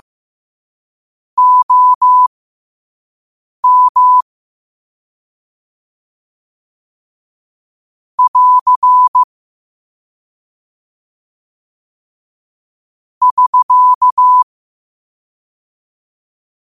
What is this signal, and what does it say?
Morse code, heard as a keyed tone, about 15 wpm, spaced as for 5 wpm, 1000 Hz. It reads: OM <AR> <SK>
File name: qotdfiles / 5wpm